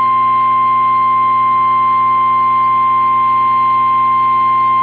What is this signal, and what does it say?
Emergency Alert System (SAME) tones (the 1050 Hz attention tone). It is heard from 0.4 to 4.5 s.
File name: eas / KWBE / KWBE